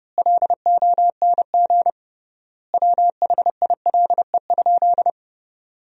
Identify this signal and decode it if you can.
Morse code, heard as a keyed tone, about 30 wpm, 700 Hz. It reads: LONG WHILE?
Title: 3x Word-Spacing - Sentences from Top 300 Words - 30wpm